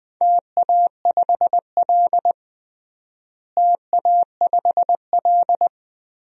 Morse code, heard as a keyed tone, about 20 wpm, 700 Hz. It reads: TA5L TA5L
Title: DXCC-100 Call Signs - 4 Characters - Repeated 2x - 20wpm